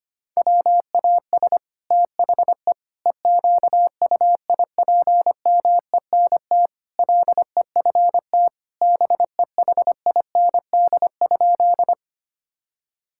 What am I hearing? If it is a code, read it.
Morse code, heard as a keyed tone, about 25 wpm, 700 Hz. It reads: WAS THE EQUIPMENT LEFT BEHIND?
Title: Sentences from Top 1000 Words - Review Part 5 - 25wpm